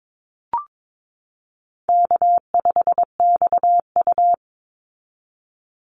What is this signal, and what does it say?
Morse code, heard as a keyed tone, about 22 wpm, 700 Hz. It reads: K5XU
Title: Easy Call Signs Repeated 3x - 22wpm